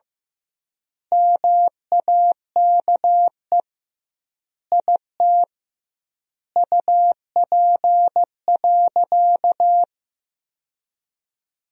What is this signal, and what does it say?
Morse code, heard as a keyed tone, about 15 wpm, 700 Hz. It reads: MAKE IT UP.
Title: 2x Word-Spacing - Sentences from Top 100 Words - 15wpm